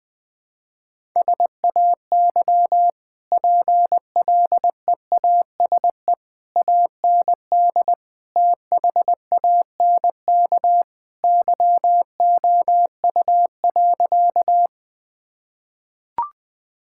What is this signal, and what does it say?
Morse code, heard as a keyed tone, about 20 wpm, 700 Hz. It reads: SAY PLEASE AND THANK YOU.
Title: Easy Sentences with Hints - 20wpm